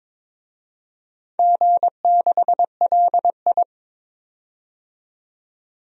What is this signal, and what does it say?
Morse code, heard as a keyed tone, about 22 wpm, 700 Hz. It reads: G6LI
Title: DXCC-100 Call Signs - 4 Characters - 22wpm